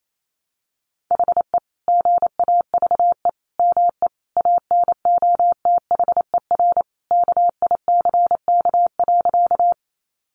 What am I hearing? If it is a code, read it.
Morse code, heard as a keyed tone, about 28 wpm, 700 Hz. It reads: HE GAVE ME ANOTHER KICK.
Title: Sentences from Top 1300 Words 28wpm